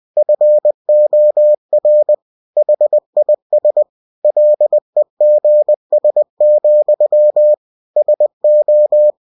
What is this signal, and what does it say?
Morse code, heard as a keyed tone, about 20 wpm, 600 Hz. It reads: FOR HIS LEGS, SO
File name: cw-books / winnie-the-pooh / winnie-20wpm-page-0093